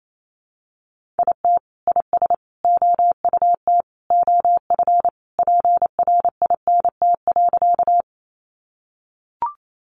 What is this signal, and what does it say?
Morse code, heard as a keyed tone, about 28 wpm, 700 Hz. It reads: IT IS OUT OF PRINT.
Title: Sentences from Top 1400 Words 28wpm